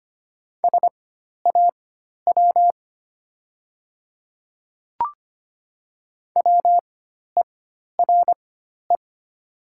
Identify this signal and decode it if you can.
Morse code, heard as a keyed tone, about 25 wpm, spaced as for 12 wpm, 700 Hz. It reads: SAW WERE
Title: 4x Character Spacing - Top 300 Words - 25wpm